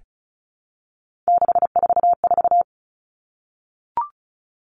Morse code, heard as a keyed tone, about 34 wpm, 700 Hz. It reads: 644
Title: Numbers - 3-Digit 35wpm